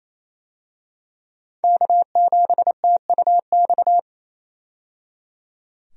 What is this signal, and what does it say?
Morse code, heard as a keyed tone, about 28 wpm, 700 Hz. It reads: K7TUX
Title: USA Call Signs - 5 Characters - 28wpm